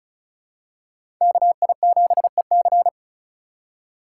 Morse code, heard as a keyed tone, about 36 wpm, 700 Hz. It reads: KI7EC
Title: USA Call Signs - 5 Characters - 4x Character Spacing - 35wpm